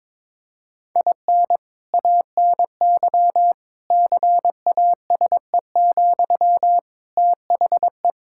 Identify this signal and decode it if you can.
Morse code, heard as a keyed tone, about 22 wpm, 700 Hz. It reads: IN ANY CASE, THE
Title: Sentences from Top 1000 Words - Review Part 5 - 22wpm